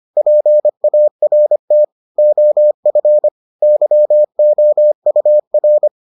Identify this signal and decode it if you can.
Morse code, heard as a keyed tone, about 25 wpm, 600 Hz. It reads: PART OF YOUR